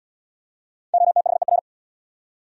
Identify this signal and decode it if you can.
Morse code, heard as a keyed tone, about 78 wpm, 700 Hz. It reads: OTHER